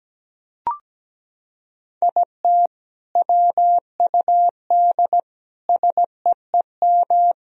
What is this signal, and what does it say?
Morse code, heard as a keyed tone, about 17 wpm, 700 Hz. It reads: IT WUD SEEM
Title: Sentences from Top 300 Words and Abbreviations 17wpm